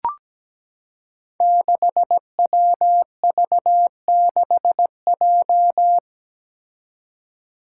Morse code, heard as a keyed tone, about 17 wpm, 700 Hz. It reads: 6WV6J
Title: Five Challenging Letter-Number Combinations 17wpm